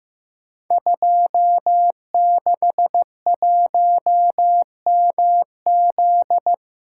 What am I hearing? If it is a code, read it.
Morse code, heard as a keyed tone, about 15 wpm, 700 Hz. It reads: 261MZ
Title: Code Groups - 5 Letter:Numbers - 15wpm